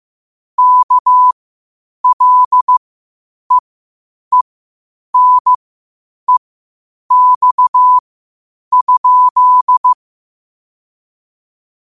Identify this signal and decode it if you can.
Morse code, heard as a keyed tone, about 15 wpm, spaced as for 8 wpm, 1000 Hz. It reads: KLEENEX?
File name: qotdfiles / 8wpm